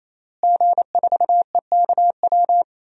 Morse code, heard as a keyed tone, about 28 wpm, 700 Hz. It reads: G4EKW
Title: DXCC-100 Call Signs - 5 Characters - Repeated 3x - 28wpm